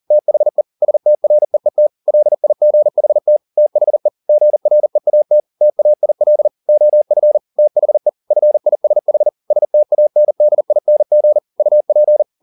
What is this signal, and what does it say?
Morse code, heard as a keyed tone, about 40 wpm, 600 Hz. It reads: THE STREET LIGHT THE GREAT TAIL OF THE FISH STANDING UP
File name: the old man and the sea 2 40wpm